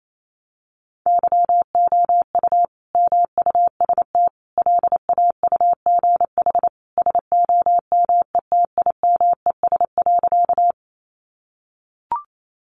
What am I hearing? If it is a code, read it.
Morse code, heard as a keyed tone, about 28 wpm, 700 Hz. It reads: YOU MUST LAUGH SOMETIMES.
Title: Sentences from Top 600 Words 28wpm